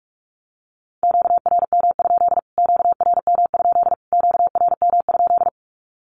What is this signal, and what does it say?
Morse code, heard as a keyed tone, about 44 wpm, 700 Hz. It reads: QRM? QRM? QRM?
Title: Q-Codes Repeated 3x 45wpm